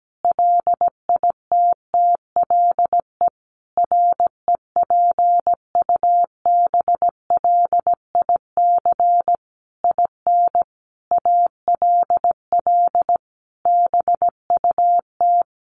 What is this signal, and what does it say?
Morse code, heard as a keyed tone, about 17 wpm, 700 Hz. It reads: LITTLE REPUBLIC IN ALL BUT
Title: Sentences from Top 1500 Words 17wpm